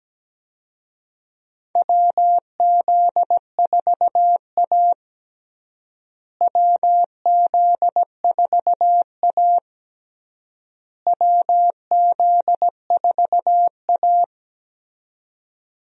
Morse code, heard as a keyed tone, about 17 wpm, 700 Hz. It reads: WZ4A WZ4A WZ4A
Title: USA Call Signs - 4 Characters - Repeated 3x - 17wpm